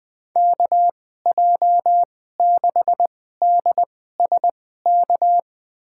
Morse code, heard as a keyed tone, about 20 wpm, spaced as for 14 wpm, 700 Hz. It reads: KJ6DSK
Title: USA Call Signs - 2x Character Spacing - 20wpm